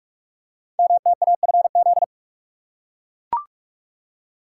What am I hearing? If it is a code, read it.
Morse code, heard as a keyed tone, about 44 wpm, 700 Hz. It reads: MTAFZ
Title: Code Groups - 5 Letters - 45wpm